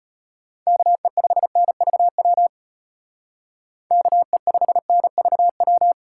Morse code, heard as a keyed tone, about 33 wpm, 700 Hz. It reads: KE5NVW KE5NVW